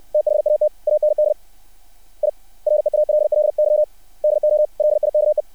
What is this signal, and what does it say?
Morse code, heard as a keyed tone, about 60 wpm, 600 Hz. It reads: THAT AND I REALLY DO LIKE
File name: archived / CRSnet